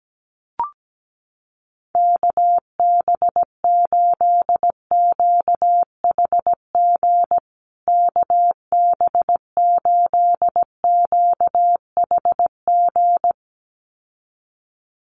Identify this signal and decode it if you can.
Morse code, heard as a keyed tone, about 17 wpm, 700 Hz. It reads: KB8QHG KB8QHG
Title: USA Call Signs Repeated 2x 17wpm